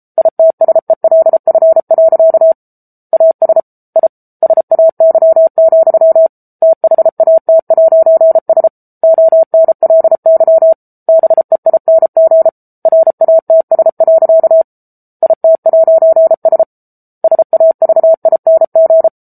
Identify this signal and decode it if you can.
Morse code, heard as a keyed tone, about 34 wpm, 650 Hz. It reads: ITSELF. AS I SAY, THAT'S ONLY BEING RATS. IT'S SAVING
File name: Book Two / 07-Chapter 7 - 33 WPM